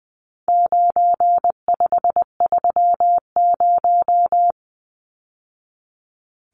Morse code, heard as a keyed tone, about 20 wpm, 700 Hz. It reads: 9530
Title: Numbers - 4-Digit 20wpm